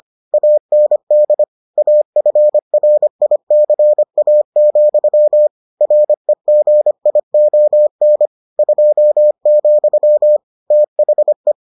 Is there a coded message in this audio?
Yes: Morse code from 0.3 to 11.5 s, heard as a keyed tone, about 25 wpm, 600 Hz. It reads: AND AFRICA, REGION 2, THE